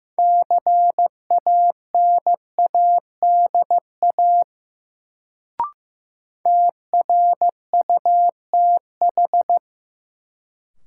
Morse code, heard as a keyed tone, about 16 wpm, 700 Hz. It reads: CANADA TRUTH
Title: Sets of 5 Words - New Word Drill - 15wpm